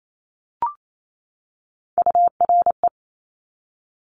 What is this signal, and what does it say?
Morse code, heard as a keyed tone, about 28 wpm, 690 Hz. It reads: URE
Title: Top Three Letter Combinations in Words 28wpm